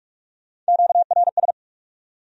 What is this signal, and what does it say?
Morse code, heard as a keyed tone, about 59 wpm, spaced as for 51 wpm, 700 Hz. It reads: QRS
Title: Speed-Racing Q-Codes - 45wpm